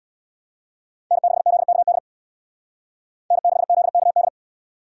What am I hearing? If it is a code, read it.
Morse code, heard as a keyed tone, about 64 wpm, 700 Hz. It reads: N6ZKD N6ZKD